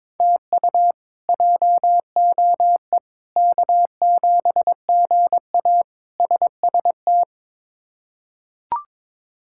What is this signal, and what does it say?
Morse code, heard as a keyed tone, about 22 wpm, 700 Hz. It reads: TU JOE K7GA SST
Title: K1USN SST QSO Practice - 22wpm